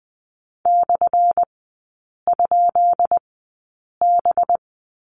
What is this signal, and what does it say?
Morse code, heard as a keyed tone, about 19 wpm, spaced as for 8 wpm, 700 Hz. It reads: /?B
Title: 191 Lesson 31e - Letter v - Three Characters 20wpm